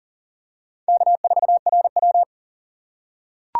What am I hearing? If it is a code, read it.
Morse code, heard as a keyed tone, about 40 wpm, 700 Hz. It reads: K4RW
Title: USA Call Signs - 2x Character Spacing - 40wpm